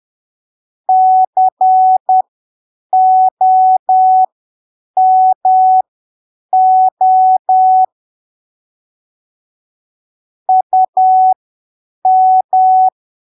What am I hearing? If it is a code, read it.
Morse code, heard as a keyed tone, about 10 wpm, spaced as for 7 wpm, 750 Hz. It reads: COMO UM
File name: arquivos / texto / texto 01 10ppm